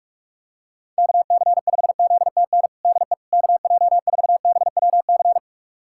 Morse code, heard as a keyed tone, about 41 wpm, 700 Hz. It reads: KC5ZTN DE KJ4BWC
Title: USA Callsign DE Callsign 45wpm